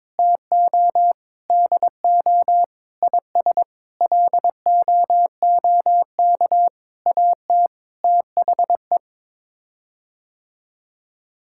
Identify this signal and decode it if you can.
Morse code, heard as a keyed tone, about 22 wpm, 700 Hz. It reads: TO DO IS LOOK AT THE
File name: Sets of 9 Words - 22wpm